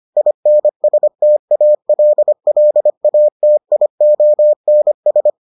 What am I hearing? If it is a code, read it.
Morse code, heard as a keyed tone, about 25 wpm, 600 Hz. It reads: INSTALLATIONS